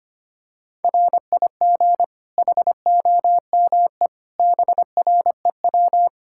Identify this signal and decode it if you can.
Morse code, heard as a keyed tone, about 25 wpm, 700 Hz. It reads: RIG HOME BREW
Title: K1USN SST QSO Practice 25wpm